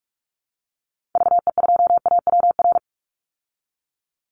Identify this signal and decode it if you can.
Morse code, heard as a keyed tone, about 44 wpm, 700 Hz. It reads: VE2AWR
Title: Call Signs Repeated 3x - 45wpm